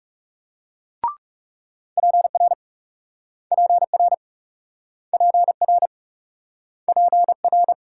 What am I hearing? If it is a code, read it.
Morse code, heard as a keyed tone, about 38 wpm, 700 Hz. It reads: PR PR PR PR